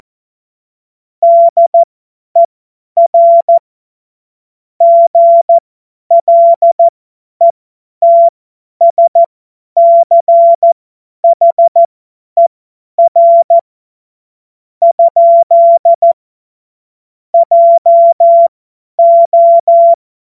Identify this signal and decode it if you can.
Morse code, heard as a keyed tone, about 14 wpm, spaced as for 9 wpm, 680 Hz. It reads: DER GLETSCHER? JO